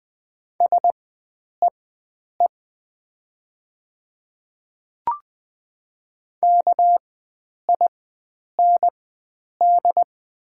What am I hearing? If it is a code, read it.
Morse code, heard as a keyed tone, about 20 wpm, spaced as for 9 wpm, 700 Hz. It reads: SEE KIND
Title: 4x Character Spacing - Top 500 Words - 20wpm